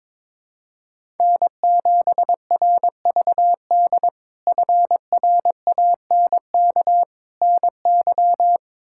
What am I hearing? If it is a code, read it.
Morse code, heard as a keyed tone, about 22 wpm, 700 Hz. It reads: N7RVD FRANK NY
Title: K1USN SST QSO Practice - 22wpm